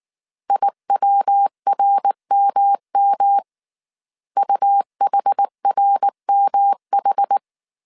Morse code, heard as a keyed tone, about 19 wpm, 800 Hz. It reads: IWRMM UHRMH